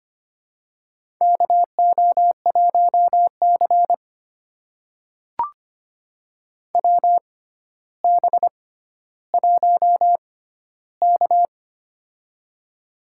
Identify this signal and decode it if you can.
Morse code, heard as a keyed tone, about 25 wpm, spaced as for 13 wpm, 700 Hz. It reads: KO1C WB1K
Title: USA Call Signs - 4 Characters - 6x Character Spacing - 25wpm